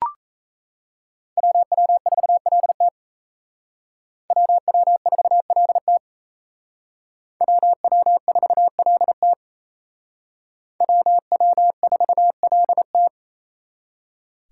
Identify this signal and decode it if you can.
Morse code, heard as a keyed tone, about 34 wpm, 700 Hz. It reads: WW4LT WW4LT WW4LT WW4LT
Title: Speed Racing - USA 2x2 Call Signs 28wpm